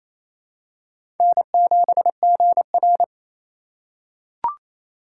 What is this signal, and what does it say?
Morse code, heard as a keyed tone, about 27 wpm, 700 Hz. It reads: N7GR
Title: USA Call Signs - 4 Characters - 6x Character Spacing - 28wpm